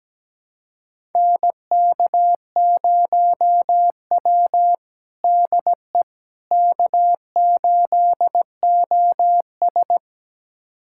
Morse code, heard as a keyed tone, about 17 wpm, 700 Hz. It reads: NK0W DE K8OS
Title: USA Callsign DE Callsign - 4 Characters - 17wpm